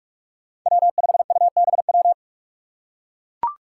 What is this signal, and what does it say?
Morse code, heard as a keyed tone, about 44 wpm, 700 Hz. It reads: W5UBW
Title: Five Challenging Letter-Number Combinations 45wpm